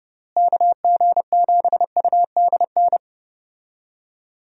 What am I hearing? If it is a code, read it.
Morse code, heard as a keyed tone, about 31 wpm, 700 Hz. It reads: KG7UDN
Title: USA Call Signs - 6 Characters - Repeated 3x - 30wpm